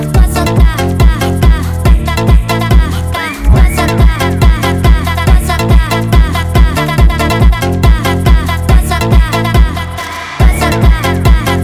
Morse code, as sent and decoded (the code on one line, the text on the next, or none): none